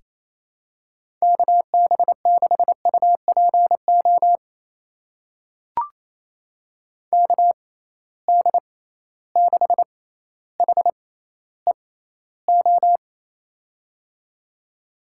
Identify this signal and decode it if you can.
Morse code, heard as a keyed tone, about 28 wpm, spaced as for 14 wpm, 700 Hz. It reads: KB6UPO KD6HEO
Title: USA Call Signs - 6 Characters - 6x Character Spacing - 28wpm